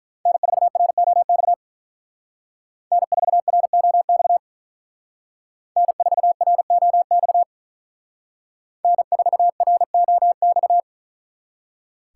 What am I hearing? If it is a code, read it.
Morse code, heard as a keyed tone, about 42 wpm, 700 Hz. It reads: N4ROX N4ROX N4ROX N4ROX
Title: Speed Racing - USA 1x3 Call Signs 35wpm